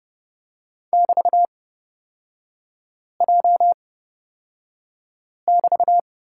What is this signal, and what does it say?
Morse code, heard as a keyed tone, about 29 wpm, spaced as for 5 wpm, 700 Hz. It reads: <BT>J<BT>
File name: Lesson 36e - BT - Three Characters 30wpm